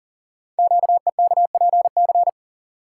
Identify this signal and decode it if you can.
Morse code, heard as a keyed tone, about 38 wpm, 700 Hz. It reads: QEKPC
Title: Code Groups - 5 Letters - 40wpm